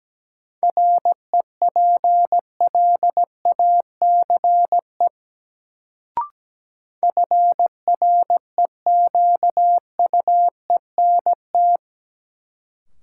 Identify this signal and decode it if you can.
Morse code, heard as a keyed tone, about 17 wpm, 700 Hz. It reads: REPLACE FREQUENT
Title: Sentences from Top 1400 Words - New Word Drill Alt - 17wpm